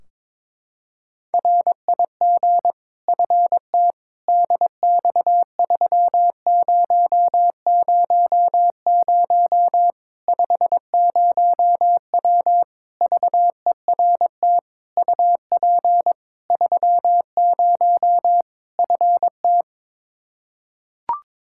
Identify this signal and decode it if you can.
Morse code, heard as a keyed tone, about 22 wpm, 700 Hz. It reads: RIG FT DX3000 50W VERT UP 30 FT
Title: Practice QSO - Exchange 2 - Info - 22wpm